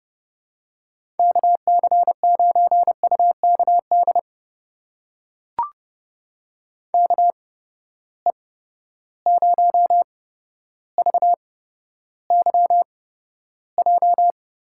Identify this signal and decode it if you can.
Morse code, heard as a keyed tone, about 30 wpm, spaced as for 13 wpm, 700 Hz. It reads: KC9UKD KE0VYJ
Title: USA Call Signs - 6 Characters - 8x Character Spacing - 30wpm